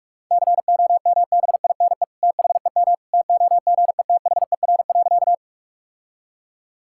Morse code, heard as a keyed tone, about 46 wpm, 700 Hz. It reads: COMBINE THEM TOGETHER.
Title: Sentences from Top 1600 Words 45wpm